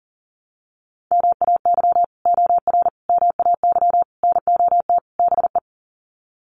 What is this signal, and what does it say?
Morse code, heard as a keyed tone, about 40 wpm, 700 Hz. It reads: MAY OR MAY NOT BE
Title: Sets of 5 Words 40wpm